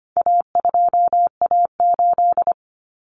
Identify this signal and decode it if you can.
Morse code, heard as a keyed tone, about 25 wpm, 700 Hz. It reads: A2A8
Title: Common Cut Numbers - 4-Digit 25wpm